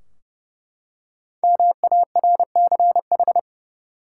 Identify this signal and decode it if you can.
Morse code, heard as a keyed tone, about 30 wpm, 700 Hz. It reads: MARCH
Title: Five Letter Words - 30wpm